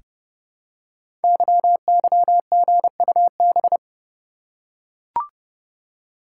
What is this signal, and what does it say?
Morse code, heard as a keyed tone, about 30 wpm, 700 Hz. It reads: YYGUB
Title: Code Groups - 5 Letters - 30wpm